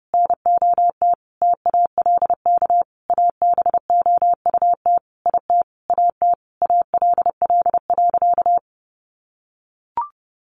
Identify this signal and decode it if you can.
Morse code, heard as a keyed tone, about 30 wpm, 700 Hz. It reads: NOT TALK ABOUT IT AT ALL.
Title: Sentences from Top 800 Words 30wpm